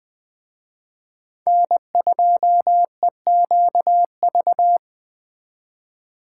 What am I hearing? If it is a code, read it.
Morse code, heard as a keyed tone, about 21 wpm, 700 Hz. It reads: N2EQV
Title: Speed Racing - USA 1x3 Call Signs 17wpm